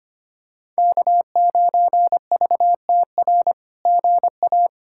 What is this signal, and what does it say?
Morse code, heard as a keyed tone, about 25 wpm, 700 Hz. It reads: K9VTR GA
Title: SOTA QSO Practice 25wpm